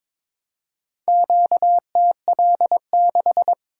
Morse code, heard as a keyed tone, about 22 wpm, 700 Hz. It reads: QTL6
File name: practice / 22wpm / Four Letter-Number Combinations - 22wpm